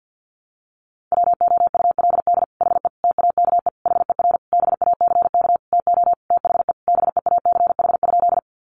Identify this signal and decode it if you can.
Morse code, heard as a keyed tone, about 50 wpm, 700 Hz. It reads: WOULD HE TAKE HER BACK TO THE BEACH?
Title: Sentences from Top 1500 Words 50wpm